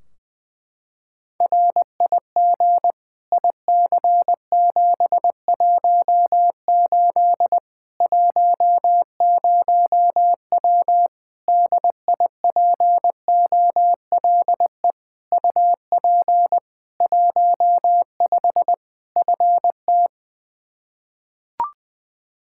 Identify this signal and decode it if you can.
Morse code, heard as a keyed tone, about 20 wpm, 700 Hz. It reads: RIG IC718 10W DIPOLE UP 15 FT
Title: Practice QSO - Exchange 2 - Info - 20wpm